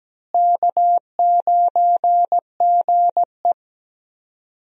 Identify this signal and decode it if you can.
Morse code, heard as a keyed tone, about 17 wpm, 700 Hz. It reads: K9GE
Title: USA Call Signs - 4 Characters - Repeated 3x - 17wpm